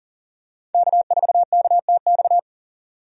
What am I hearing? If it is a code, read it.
Morse code, heard as a keyed tone, about 38 wpm, 700 Hz. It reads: K4KTX
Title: USA Call Signs - 5 Characters - 40wpm